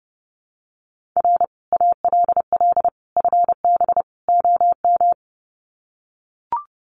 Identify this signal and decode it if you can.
Morse code, heard as a keyed tone, about 30 wpm, 700 Hz. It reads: R ALL FB OM
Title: Common QSO Phrases 30wpm